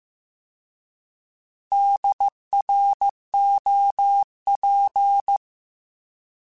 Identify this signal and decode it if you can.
Morse code, heard as a keyed tone, about 15 wpm, 800 Hz. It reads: DROP
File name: Alphabetical / Alphabetical mp3s / Ds @13